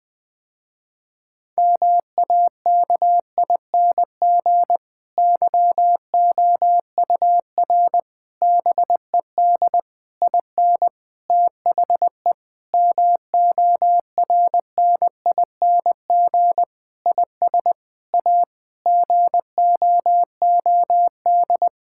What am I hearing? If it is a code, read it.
Morse code, heard as a keyed tone, about 20 wpm, 700 Hz. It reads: MAKING YOUR BED IN THE MORNING IS A GOOD
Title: Difficult Sentences with Hints - 20wpm